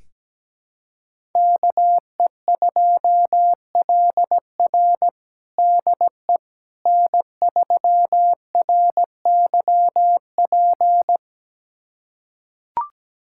Morse code, heard as a keyed tone, about 17 wpm, 700 Hz. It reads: KE2LR DE N3RYP
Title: USA Callsign DE Callsign - 5 Characters - 17wpm